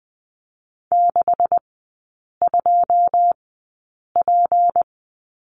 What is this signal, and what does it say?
Morse code, heard as a keyed tone, about 20 wpm, spaced as for 8 wpm, 700 Hz. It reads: 62P
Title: 152 Lesson 25e - Letter p - Three Characters 20wpm